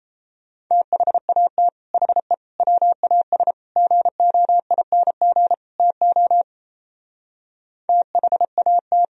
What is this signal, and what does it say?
Morse code, heard as a keyed tone, about 33 wpm, 700 Hz. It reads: THAT HE WAS GOING TO THAT
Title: Speed-Racing - Sets of 5 Words 28wpm